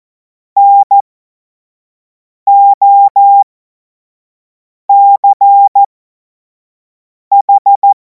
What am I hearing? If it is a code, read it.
Morse code, heard as a keyed tone, about 14 wpm, spaced as for 5 wpm, 800 Hz. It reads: NOCH
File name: Morsekurs / Witze / Witz 28. April 2026 25 F70